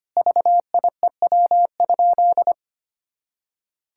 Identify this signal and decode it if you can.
Morse code, heard as a keyed tone, about 25 wpm, 700 Hz. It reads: VIEW?
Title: Sentences from Top 1000 Words - Review Part 6 - 25wpm